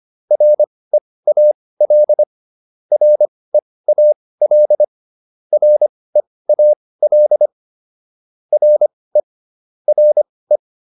Morse code, heard as a keyed tone, about 25 wpm, spaced as for 18 wpm, 600 Hz. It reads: REAL REAL REAL RE RE
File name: RE4 18